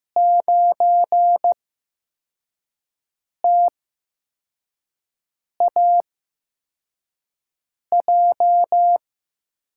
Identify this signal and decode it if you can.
Morse code, heard as a keyed tone, about 15 wpm, spaced as for 4 wpm, 700 Hz. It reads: 9TAJ
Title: USA Call Signs - 6 Characters - 8x Character Spacing - 15wpm